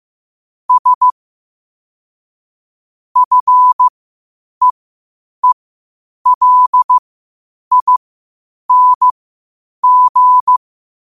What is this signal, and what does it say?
Morse code, heard as a keyed tone, about 15 wpm, spaced as for 8 wpm, 1000 Hz. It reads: S FEELING
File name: qotdfiles / 8wpm